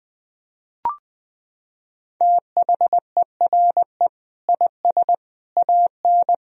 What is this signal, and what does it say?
Morse code, heard as a keyed tone, about 20 wpm, 700 Hz. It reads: THERE IS AN
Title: Sentences from Top 1000 Words - Review Part 3 - 20wpm